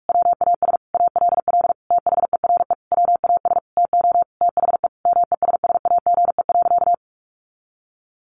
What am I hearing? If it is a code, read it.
Morse code, heard as a keyed tone, about 44 wpm, 700 Hz. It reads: WAS ALL THERE WAS TO THE MESSAGE.
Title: Sentences from Top 1100 Words 45wpm